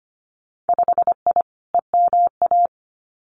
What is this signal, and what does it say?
Morse code, heard as a keyed tone, about 25 wpm, 700 Hz. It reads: 5I EMA
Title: Winter Field Day Exchanges Encore 25wpm